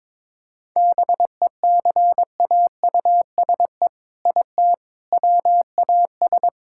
Morse code, heard as a keyed tone, about 22 wpm, 700 Hz. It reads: BECAUSE IT WAS
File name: All-Age Appropriate Jokes - 22wpm